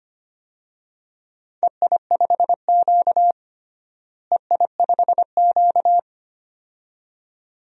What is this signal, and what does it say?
Morse code, heard as a keyed tone, about 25 wpm, 700 Hz. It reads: EI5Q EI5Q
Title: DXCC-100 Call Signs - 4 Characters - Repeated 2x - 25wpm